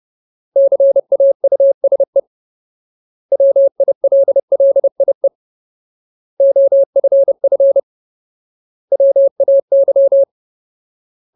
Morse code, heard as a keyed tone, about 30 wpm, 550 Hz. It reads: CAUSE WILLIE OFF WAY
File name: Ultimate Word List / 30WPM Ultimate